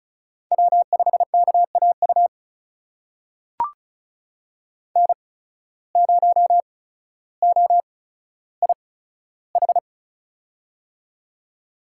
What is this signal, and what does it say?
Morse code, heard as a keyed tone, about 36 wpm, spaced as for 15 wpm, 700 Hz. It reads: W5KAU N0OIH
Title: USA Call Signs - 5 Characters - 8x Character Spacing - 35wpm